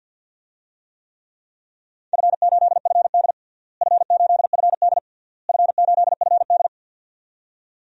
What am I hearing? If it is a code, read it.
Morse code, heard as a keyed tone, about 52 wpm, 700 Hz. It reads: F8FD F8FD F8FD
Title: DXCC-100 Call Signs - 4 Characters - Repeated 3x - 50wpm